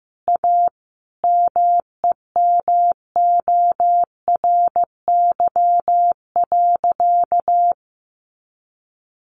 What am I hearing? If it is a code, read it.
Morse code, heard as a keyed tone, about 14 wpm, 700 Hz. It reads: A MEMORY.
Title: Sentences from Top 1300 Words 15wpm